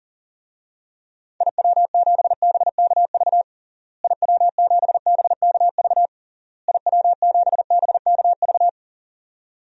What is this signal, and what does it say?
Morse code, heard as a keyed tone, about 37 wpm, 700 Hz. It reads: IW7BKV IW7BKV IW7BKV
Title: DXCC-100 Call Signs - 6 Characters - Repeated 3x - 40wpm